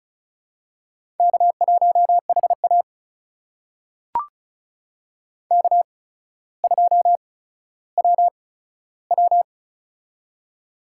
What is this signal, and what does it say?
Morse code, heard as a keyed tone, about 36 wpm, spaced as for 15 wpm, 700 Hz. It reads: K1HA K2WW
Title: USA Call Signs - 4 Characters - 8x Character Spacing - 35wpm